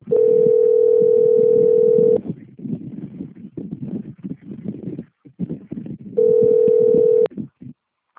Ringback tone (North America), 440 + 480 Hz.